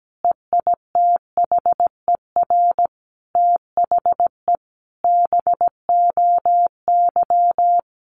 Morse code, heard as a keyed tone, about 17 wpm, 700 Hz. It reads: EITHER THE BOY
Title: Sentences from Top 1500 Words 17wpm